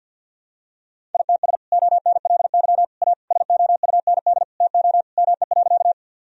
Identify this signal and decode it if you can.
Morse code, heard as a keyed tone, about 52 wpm, 700 Hz. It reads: ITS ONLY A SOUND TO ME.
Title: Sentences from Top 1000 Words - Review Part 1 - 50wpm